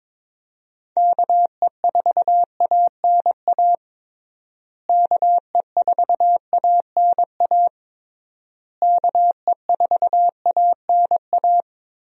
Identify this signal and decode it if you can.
Morse code, heard as a keyed tone, about 21 wpm, 700 Hz. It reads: KE4ANA KE4ANA KE4ANA
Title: USA Call Signs - 6 Characters - Repeated 3x - 22wpm